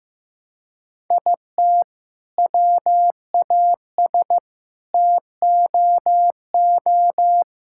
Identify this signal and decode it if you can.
Morse code, heard as a keyed tone, about 15 wpm, 700 Hz. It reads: IT WAS TOO
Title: Sentences from Top 1600 Words 15wpm